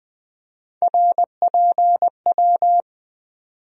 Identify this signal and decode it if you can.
Morse code, heard as a keyed tone, about 20 wpm, 700 Hz. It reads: RPW
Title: USA Call Sign Postfixes - 3 Letters 20wpm